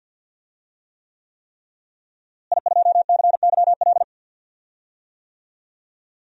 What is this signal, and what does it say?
Morse code, heard as a keyed tone, about 47 wpm, 700 Hz. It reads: I2CYL
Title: DXCC-100 Call Signs - 5 Characters - Repeated 3x - 50wpm